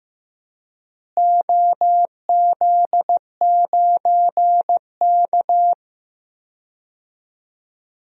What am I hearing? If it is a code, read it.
Morse code, heard as a keyed tone, about 15 wpm, 700 Hz. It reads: OZ9K
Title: DXCC-100 Call Signs - 4 Characters - Repeated 3x - 15wpm